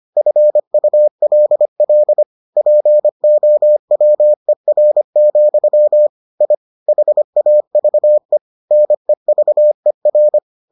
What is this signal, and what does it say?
Morse code, heard as a keyed tone, about 25 wpm, 600 Hz. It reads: FULL POWER, I HAVE NEVER